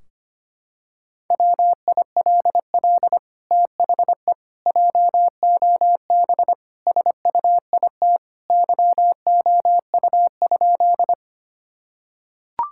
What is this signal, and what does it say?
Morse code, heard as a keyed tone, about 25 wpm, 700 Hz. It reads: WILL THE JOB SUIT YOU?
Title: Sentences from Top 1600 Words 25wpm